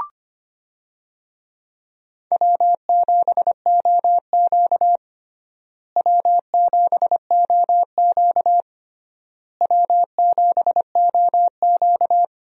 Morse code, heard as a keyed tone, about 25 wpm, 700 Hz. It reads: W7OQ W7OQ W7OQ
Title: USA Call Signs - 4 Characters - Repeated 3x - 25wpm